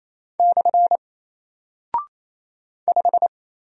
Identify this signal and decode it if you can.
Morse code, heard as a keyed tone, about 28 wpm, 700 Hz. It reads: / 5
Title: Single Character (Letter:Number:Punctuation:Prosign) Alt 28wpm